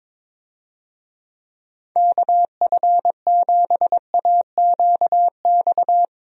Morse code, heard as a keyed tone, about 22 wpm, 700 Hz. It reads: KF7AQX
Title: USA Call Signs - 6 Characters - Repeated 3x - 22wpm